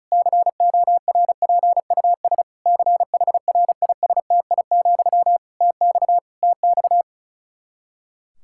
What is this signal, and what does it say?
Morse code, heard as a keyed tone, about 36 wpm, 700 Hz. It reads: CORPUS CHRISTI, TX TX
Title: K1USN SST QSO Practice 35wpm